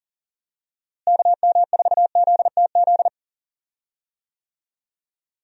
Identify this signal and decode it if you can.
Morse code, heard as a keyed tone, about 40 wpm, 700 Hz. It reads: KM4ZTZ
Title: USA Call Signs - 6 Characters - Repeated 3x - 40wpm